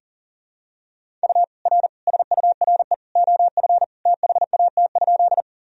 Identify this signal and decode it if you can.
Morse code, heard as a keyed tone, about 40 wpm, 700 Hz. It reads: U R SURE OF THAT?
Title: Sentences from Top 300 Words and Abbreviations 40wpm